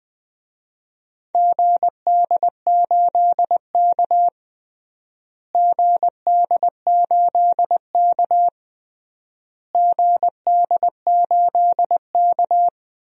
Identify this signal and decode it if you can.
Morse code, heard as a keyed tone, about 20 wpm, 700 Hz. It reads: GD8K GD8K GD8K
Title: DXCC-100 Call Signs - 4 Characters - Repeated 3x - 20wpm